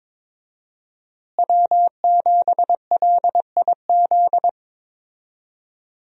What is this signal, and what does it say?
Morse code, heard as a keyed tone, about 22 wpm, 700 Hz. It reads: W7LIZ
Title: USA Call Signs - 5 Characters - Repeated 2x - 22wpm